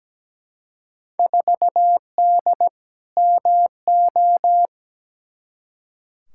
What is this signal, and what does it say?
Morse code, heard as a keyed tone, about 17 wpm, 700 Hz. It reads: 4D MO
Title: Field Day Exchanges Encore - 17wpm